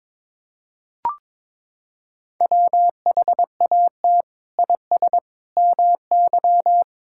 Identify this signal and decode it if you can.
Morse code, heard as a keyed tone, about 23 wpm, 700 Hz. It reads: WHAT IS MY
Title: Sentences from Top 1000 Words - Review Part 6 - 22wpm